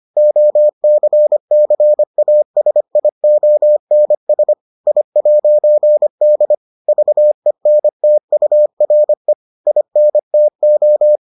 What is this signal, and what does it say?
Morse code, heard as a keyed tone, about 25 wpm, 600 Hz. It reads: OCCASIONS I'D VENTURE INTO